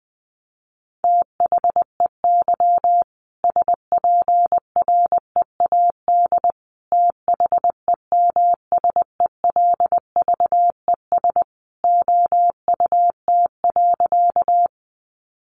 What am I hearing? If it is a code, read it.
Morse code, heard as a keyed tone, about 20 wpm, 700 Hz. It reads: THEY SPREAD THEMSELVES OUT.
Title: Sentences from Top 1200 Words 20wpm